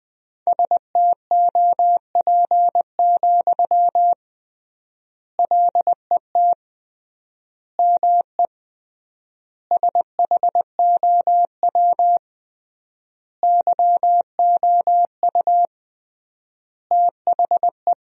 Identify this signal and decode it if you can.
Morse code, heard as a keyed tone, about 21 wpm, 700 Hz. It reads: STOP, LET ME SHOW YOU THE
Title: 3x Word-Spacing - Sentences from Top 300 Words - 20wpm